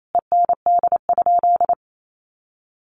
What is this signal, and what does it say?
Morse code, heard as a keyed tone, about 28 wpm, 700 Hz. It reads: END?
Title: Sentences from Top 1200 Words 28wpm